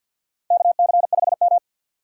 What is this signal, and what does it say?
Morse code, heard as a keyed tone, about 51 wpm, 700 Hz. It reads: KC5M
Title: USA Call Signs - 4 Characters - 6x Character Spacing - 50wpm